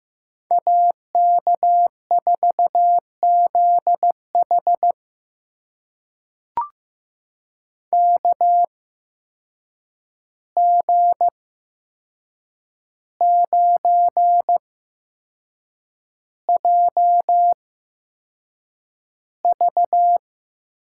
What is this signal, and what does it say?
Morse code, heard as a keyed tone, about 15 wpm, spaced as for 6 wpm, 700 Hz. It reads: AK4ZH KG9JV
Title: USA Call Signs - 5 Characters - 8x Character Spacing - 15wpm